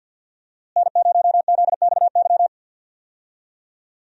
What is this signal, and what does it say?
Morse code, heard as a keyed tone, about 52 wpm, 700 Hz. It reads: N0ZXY
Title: USA Call Signs - 5 Characters - Repeated 3x - 50wpm